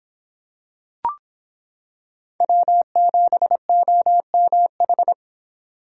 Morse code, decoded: W7OMH

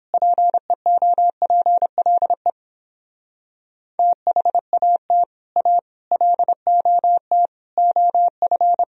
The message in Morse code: PEOPLE THAT A LOT OF

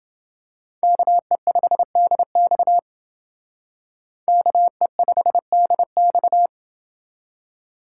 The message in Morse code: KE5DX KE5DX

